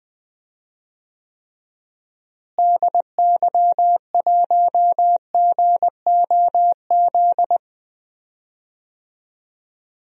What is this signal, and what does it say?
Morse code, heard as a keyed tone, about 20 wpm, 700 Hz. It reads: DY1GOZ
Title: DXCC-100 Call Signs - 6 Characters - Repeated 2x - 20wpm